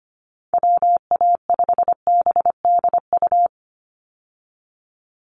Morse code, heard as a keyed tone, about 25 wpm, 700 Hz. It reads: WA5BDU
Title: Call Signs Repeated 2x - 25wpm